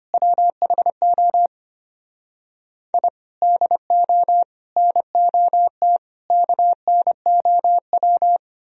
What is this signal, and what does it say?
Morse code, heard as a keyed tone, about 33 wpm, 700 Hz. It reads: WHO I DO NOT KNOW